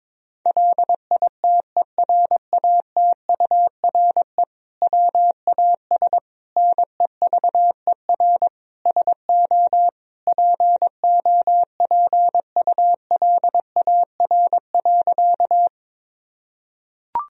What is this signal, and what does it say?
Morse code, heard as a keyed tone, about 22 wpm, 700 Hz. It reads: LITERATURE WAS NEVER SO POPULAR.
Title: Sentences from Top 1000 Words - Review Part 3 - 22wpm